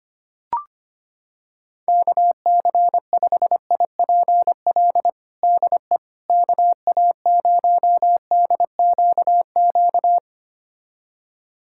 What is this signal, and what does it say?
Morse code, heard as a keyed tone, about 25 wpm, 700 Hz. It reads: KC5IPL DE KA0DQQ
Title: USA Callsign DE Callsign - 6 Characters - 25wpm